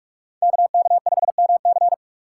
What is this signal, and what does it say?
Morse code, heard as a keyed tone, about 40 wpm, 700 Hz. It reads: KK5MC